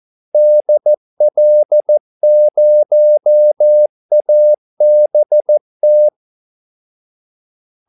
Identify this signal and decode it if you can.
Morse code, heard as a keyed tone, about 14 wpm, 600 Hz. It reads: DL0ABT
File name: Callsigns-14wpm